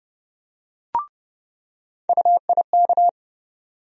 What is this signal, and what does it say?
Morse code, heard as a keyed tone, about 30 wpm, 700 Hz. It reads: UIK
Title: USA Call Sign Postfixes - 3 Letters 30wpm